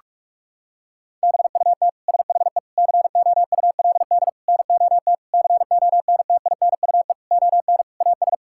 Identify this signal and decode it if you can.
Morse code, heard as a keyed tone, about 41 wpm, 700 Hz. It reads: BUT SHE COULD NOT CONTINUE ON AS
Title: Sentences from Top 1000 Words - Review Part 6 - 45wpm